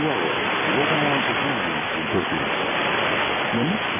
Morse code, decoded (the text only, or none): none